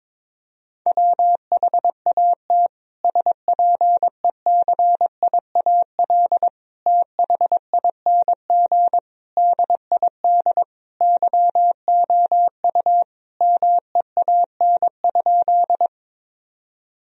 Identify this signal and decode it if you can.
Morse code, heard as a keyed tone, about 22 wpm, 700 Hz. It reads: WHAT SPECIAL THING DID YOU MEAN?
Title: Sentences from Top 1000 Words - Review Part 3 - 22wpm